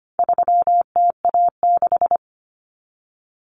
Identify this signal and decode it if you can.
Morse code, heard as a keyed tone, about 25 wpm, 700 Hz. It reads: 3TA6
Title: Common Cut Numbers - 4-Digit 25wpm